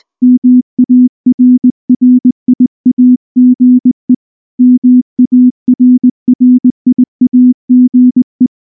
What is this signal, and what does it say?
Morse code, heard as a keyed tone, about 19 wpm, 250 Hz. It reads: MARRIAGE MARRIAGE
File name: morse / morse-training-20WPM-5